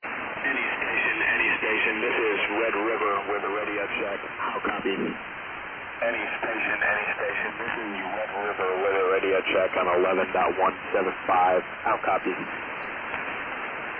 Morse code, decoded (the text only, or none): none